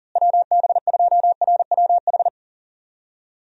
Morse code, heard as a keyed tone, about 37 wpm, 700 Hz. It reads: WB2RWH
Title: USA Call Signs - 6 Characters - Repeated 3x - 40wpm